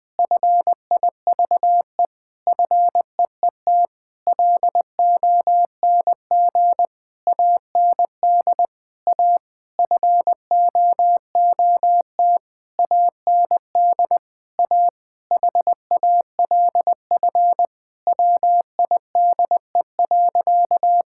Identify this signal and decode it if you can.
Morse code, heard as a keyed tone, about 20 wpm, 700 Hz. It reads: FIVE FEET LONG AND A FOOT AND A HALF WIDE.
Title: Sentences from Top 1000 Words - Review Part 6 - 20wpm